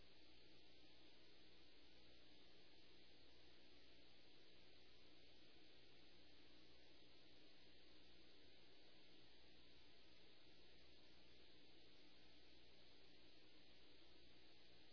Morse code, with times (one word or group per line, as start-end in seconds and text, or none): none